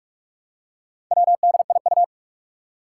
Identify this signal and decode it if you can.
Morse code, heard as a keyed tone, about 44 wpm, 700 Hz. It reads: WDIU
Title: Four Challenging Letter-Number Combinations 45wpm